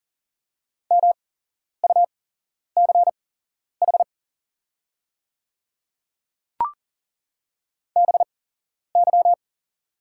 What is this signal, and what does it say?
Morse code, heard as a keyed tone, about 40 wpm, spaced as for 11 wpm, 700 Hz. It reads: MUCH BY
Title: 8x Character Spacing - Top 200 Words - 40wpm